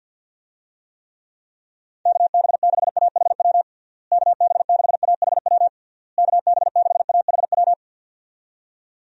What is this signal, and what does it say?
Morse code, heard as a keyed tone, about 51 wpm, 700 Hz. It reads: KB6AHW KB6AHW KB6AHW
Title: USA Call Signs - 6 Characters - Repeated 3x - 50wpm